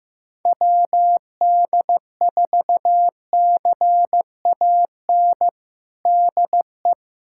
Morse code, heard as a keyed tone, about 15 wpm, 700 Hz. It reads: WD4CAN DE
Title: USA Callsign DE Callsign 15wpm